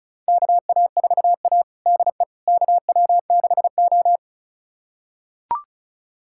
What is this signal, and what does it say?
Morse code, heard as a keyed tone, about 36 wpm, 700 Hz. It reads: KA4A DE KW6O
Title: USA Callsign DE Callsign - 4 Characters - 35wpm